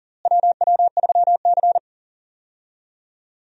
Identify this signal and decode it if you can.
Morse code, heard as a keyed tone, about 40 wpm, 700 Hz. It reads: WW3C
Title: USA Call Signs - 4 Characters - Repeated 3x - 40wpm